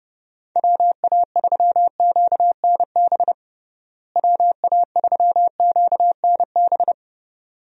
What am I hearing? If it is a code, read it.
Morse code, heard as a keyed tone, about 30 wpm, 700 Hz. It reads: WA3QNB WA3QNB
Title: USA Call Signs - 6 Characters - Repeated 2x - 30wpm